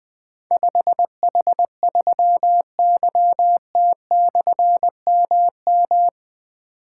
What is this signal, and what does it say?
Morse code, heard as a keyed tone, about 21 wpm, 700 Hz. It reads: 5H3YT/MM
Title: Difficult Call Signs 20wpm